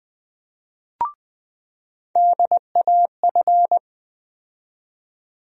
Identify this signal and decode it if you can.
Morse code, heard as a keyed tone, about 20 wpm, 700 Hz. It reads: DAF
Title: USA Call Sign Postfixes - 3 Letters 20wpm